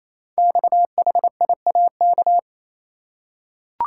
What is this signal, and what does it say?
Morse code, heard as a keyed tone, about 27 wpm, 700 Hz. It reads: XHIAK
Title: Code Groups - 5 Letter:Numbers - 28wpm